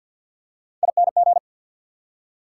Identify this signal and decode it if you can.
Morse code, heard as a keyed tone, about 50 wpm, 690 Hz. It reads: ING
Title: Prefixes and Suffixes 50wpm